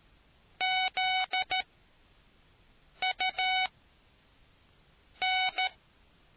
Morse code, heard as a keyed tone, about 13 wpm, spaced as for 5 wpm, 750 Hz. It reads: ZUN